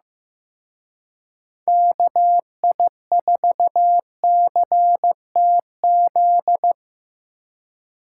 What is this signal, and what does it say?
Morse code, heard as a keyed tone, about 17 wpm, 700 Hz. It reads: KI4CTZ
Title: Speed Racing - USA 2x3 Call Signs 15wpm